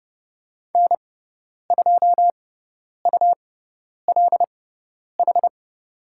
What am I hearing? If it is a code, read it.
Morse code, heard as a keyed tone, about 30 wpm, spaced as for 10 wpm, 700 Hz. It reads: N2ULH
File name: Lesson 18g - Letter m - Call Signs 30wpm